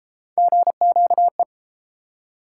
Morse code, heard as a keyed tone, about 30 wpm, 700 Hz. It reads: GQE